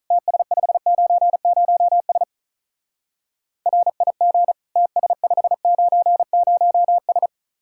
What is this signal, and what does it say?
Morse code, heard as a keyed tone, about 44 wpm, 700 Hz. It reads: TS590S RIG TS590S